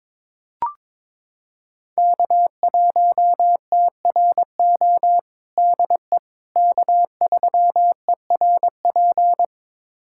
Morse code, heard as a keyed tone, about 22 wpm, 700 Hz. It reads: K1TRO DE K3ERP
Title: USA Callsign DE Callsign 22wpm